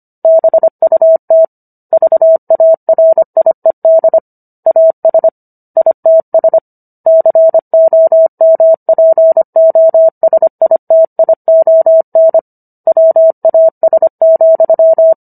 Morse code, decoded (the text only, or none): BUT VARIED AS ITS COMPOSITION WAS,